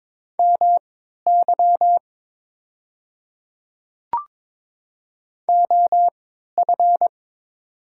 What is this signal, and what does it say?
Morse code, heard as a keyed tone, about 22 wpm, spaced as for 12 wpm, 700 Hz. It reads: MY OF
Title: 3x Character Spacing - Top 100 Words - 22wpm